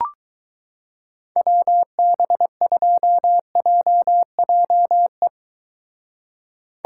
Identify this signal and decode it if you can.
Morse code, heard as a keyed tone, about 22 wpm, 700 Hz. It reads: WB2JJE